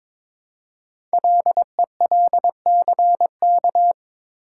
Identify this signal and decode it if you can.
Morse code, heard as a keyed tone, about 22 wpm, 700 Hz. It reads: LELCK
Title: Code Groups - 5 Letters - 22wpm